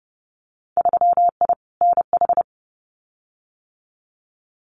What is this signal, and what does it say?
Morse code, heard as a keyed tone, about 30 wpm, 700 Hz. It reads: 3I NH
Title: Winter Field Day Exchanges Encore 30wpm